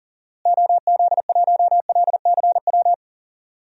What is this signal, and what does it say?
Morse code, heard as a keyed tone, about 40 wpm, 700 Hz. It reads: OZ1LCW
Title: DXCC-100 Call Signs - 6 Characters - Repeated 2x - 40wpm